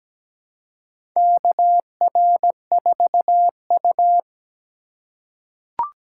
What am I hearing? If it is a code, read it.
Morse code, heard as a keyed tone, about 17 wpm, 700 Hz. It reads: KR4U
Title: USA Call Signs - 4 Characters - 3x Character Spacing - 17wpm